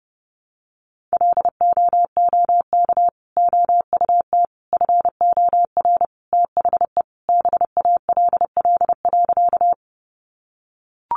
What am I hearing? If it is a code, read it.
Morse code, heard as a keyed tone, about 30 wpm, 700 Hz. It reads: LOOK OUT FOR THE BALL.
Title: Sentences from Top 900 Words 30wpm